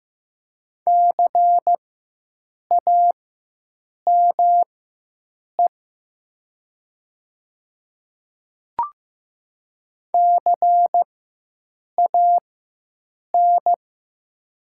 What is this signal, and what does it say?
Morse code, heard as a keyed tone, about 15 wpm, spaced as for 7 wpm, 700 Hz. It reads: CAME CAN
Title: 4x Character Spacing - Top 500 Words - 15wpm